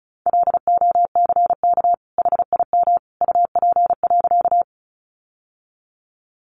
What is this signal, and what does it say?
Morse code, heard as a keyed tone, about 34 wpm, 700 Hz. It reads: LOCK HIM UP.
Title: Sentences from Top 1300 Words 35wpm